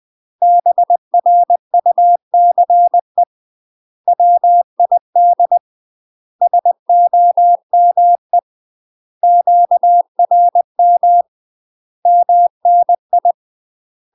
Morse code, heard as a keyed tone, about 20 wpm, 700 Hz. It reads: BRUCE WID SOME QRM MNI
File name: UK MASTER QSOs TRAINING / 20WPM / UK MASTER QSOs 0016